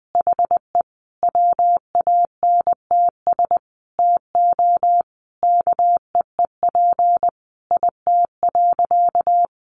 Morse code, heard as a keyed tone, about 20 wpm, 700 Hz. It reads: HE WANTS TO KEEP IT.
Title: Additional Easy Phrases 20wpm